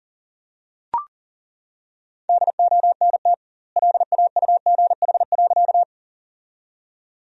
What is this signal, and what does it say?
Morse code, heard as a keyed tone, about 40 wpm, 700 Hz. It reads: DONT LAUGH.
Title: 2x Word-Spacing - Sentences from Top 500 Words - 40wpm